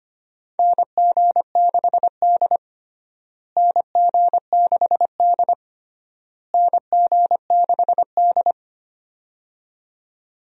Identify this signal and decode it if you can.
Morse code, heard as a keyed tone, about 25 wpm, 700 Hz. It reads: NG6D NG6D NG6D
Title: USA Call Signs - 4 Characters - Repeated 3x - 25wpm